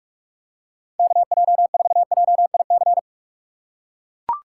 Morse code, heard as a keyed tone, about 43 wpm, 700 Hz. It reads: KJ4JIC
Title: USA Call Signs - 6x Character Spacing - 45wpm